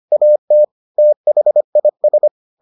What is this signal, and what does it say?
Morse code, heard as a keyed tone, about 25 wpm, 600 Hz. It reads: AT THIS